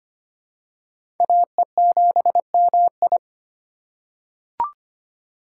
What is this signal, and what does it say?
Morse code, heard as a keyed tone, about 25 wpm, 700 Hz. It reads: AE7MI
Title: USA Call Signs - 5 Characters - 4x Character Spacing - 25wpm